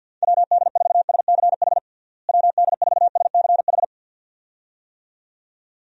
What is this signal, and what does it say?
Morse code, heard as a keyed tone, about 51 wpm, 700 Hz. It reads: WD4SCH WD4SCH
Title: USA Call Signs - 6 Characters - Repeated 2x - 50wpm